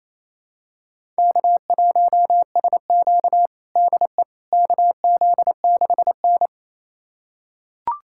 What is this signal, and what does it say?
Morse code, heard as a keyed tone, about 28 wpm, 700 Hz. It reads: K1SQ DE KZ6N
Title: USA Callsign DE Callsign - 4 Characters - 28wpm